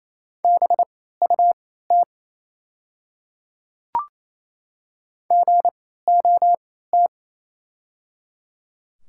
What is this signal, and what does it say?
Morse code, heard as a keyed tone, about 28 wpm, spaced as for 16 wpm, 700 Hz. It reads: BUT GOT
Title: 3x Character Spacing - Top 300 Words - 28wpm